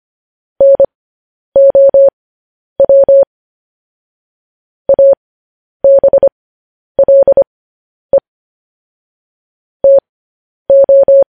Morse code, heard as a keyed tone, about 25 wpm, spaced as for 10 wpm, 550 Hz. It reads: NOW ABLE TO